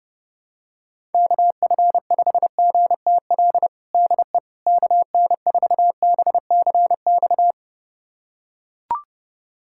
Morse code, heard as a keyed tone, about 32 wpm, 700 Hz. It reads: KF5GTL DE KN4BCX
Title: USA Callsign DE Callsign 30wpm